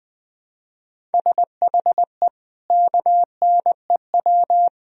Morse code, heard as a keyed tone, about 20 wpm, 700 Hz. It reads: SHE KNEW
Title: Sentences from Top 400 Words and Abbreviations 20wpm